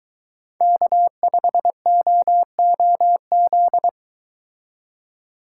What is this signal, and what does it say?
Morse code, heard as a keyed tone, about 23 wpm, 700 Hz. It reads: K5OOZ